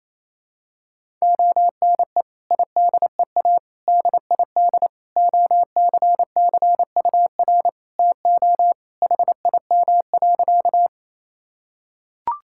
Morse code, heard as a keyed tone, about 28 wpm, 700 Hz. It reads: ONE IDEA DID OCCUR TO HIM.
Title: Sentences from Top 1000 Words - Review Part 6 - 28wpm